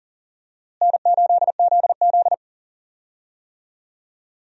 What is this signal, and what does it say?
Morse code, heard as a keyed tone, about 37 wpm, 700 Hz. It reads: N8ZZ
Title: USA Call Signs - 4 Characters - Repeated 2x - 40wpm